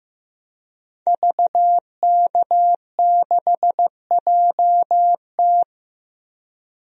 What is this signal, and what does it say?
Morse code, heard as a keyed tone, about 15 wpm, 700 Hz. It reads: VK6JT
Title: DXCC-100 Call Signs - 5 Characters - Repeated 2x - 15wpm